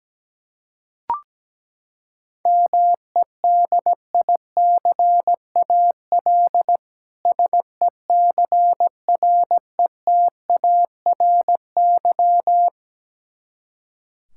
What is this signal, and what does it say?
Morse code, heard as a keyed tone, about 17 wpm, 700 Hz. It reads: MEDICAL SECRETARY
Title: K1USN SST QSO Practice 17wpm